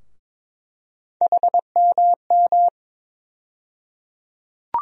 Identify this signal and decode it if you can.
Morse code, heard as a keyed tone, about 22 wpm, 700 Hz. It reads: HMM
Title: USA Call Sign Postfixes - 3 Letters 22wpm